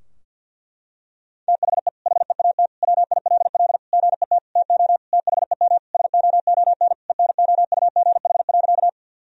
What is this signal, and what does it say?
Morse code, heard as a keyed tone, about 52 wpm, 700 Hz. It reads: THE HEAT WILL GET TO THEM SOON ENOUGH.
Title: Sentences from Top 1000 Words - Review Part 3 - 50wpm